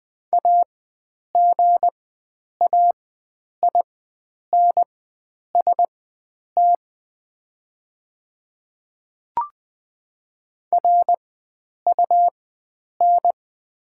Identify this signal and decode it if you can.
Morse code, heard as a keyed tone, about 20 wpm, spaced as for 9 wpm, 700 Hz. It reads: AGAINST RUN
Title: 4x Character Spacing - Top 500 Words - 20wpm